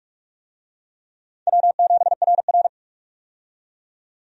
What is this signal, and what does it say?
Morse code, heard as a keyed tone, about 46 wpm, 700 Hz. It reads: W7RR
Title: USA Call Signs - 4 Characters - 45wpm